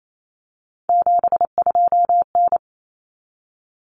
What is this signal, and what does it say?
Morse code, heard as a keyed tone, about 28 wpm, 700 Hz. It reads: 72N
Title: Common Cut Numbers - 3-Digit 28wpm